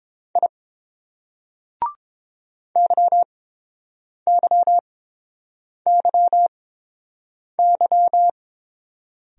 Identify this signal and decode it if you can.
Morse code, heard as a keyed tone, about 29 wpm, spaced as for 7 wpm, 700 Hz. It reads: I YYYY